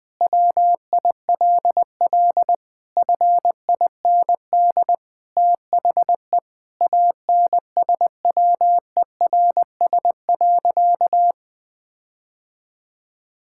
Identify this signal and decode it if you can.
Morse code, heard as a keyed tone, about 20 wpm, 700 Hz. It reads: WILL FIND THE ANSWERS.
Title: Sentences with 5-Word Phrases - Spoken Beginning - 20wpm